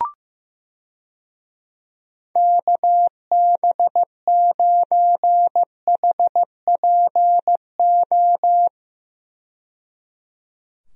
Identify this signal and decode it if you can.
Morse code, heard as a keyed tone, about 15 wpm, 700 Hz. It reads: KB9HPO
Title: USA Call Signs - 6 Characters - 15wpm